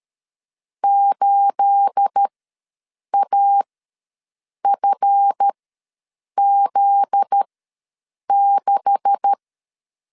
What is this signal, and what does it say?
Morse code, heard as a keyed tone, about 13 wpm, spaced as for 7 wpm, 800 Hz. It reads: 8AFZ6
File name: LEZIONI / 9 286 ws 12wpm 3351